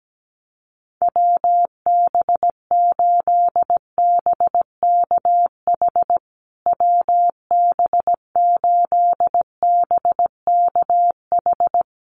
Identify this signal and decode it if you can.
Morse code, heard as a keyed tone, about 17 wpm, 700 Hz. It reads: WB8BKH WB8BKH
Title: USA Call Signs Repeated 3x 17wpm